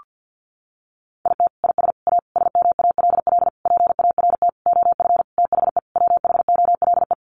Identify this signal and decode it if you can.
Morse code, heard as a keyed tone, about 50 wpm, 700 Hz. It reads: IT IS A SMALL PART OF THE WHOLE
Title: Sentences from Top 1500 Words 50wpm